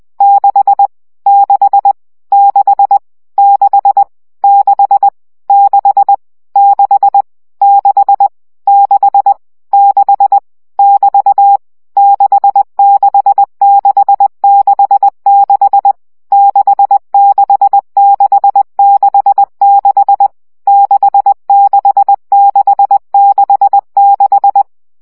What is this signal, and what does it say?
Morse code, heard as a keyed tone, about 20 wpm, 800 Hz. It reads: 6 6 6 6 6 6 6 6 6 6 <BT> 66666 66666 66666
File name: CW gyakorlas 6 17 baud